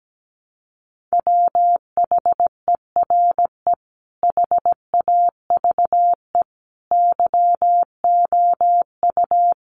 Morse code, heard as a keyed tone, about 17 wpm, 700 Hz. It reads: WHERE HAVE YOU
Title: Sentences from Top 300 Words 17wpm